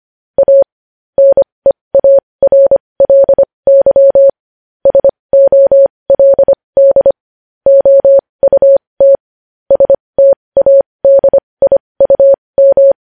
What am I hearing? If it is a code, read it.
Morse code, heard as a keyed tone, about 25 wpm, spaced as for 20 wpm, 550 Hz. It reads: A NEARLY SOLD OUT STADIUM
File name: podcast / News headlines-1x-25x20